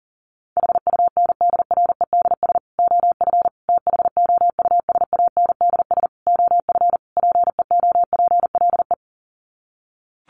Morse code, heard as a keyed tone, about 40 wpm, 700 Hz. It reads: HUNDREDS OF THOUSANDS OF PEOPLE
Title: Sets of 5 Words Encore2 40wpm